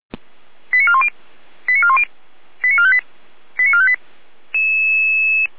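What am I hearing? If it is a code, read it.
Five-tone selective call (ZVEI), heard as quick runs of five tones, tones of about 70 ms. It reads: 89410 89410 89458 89458
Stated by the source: ZVEI five-tone call received on SDR.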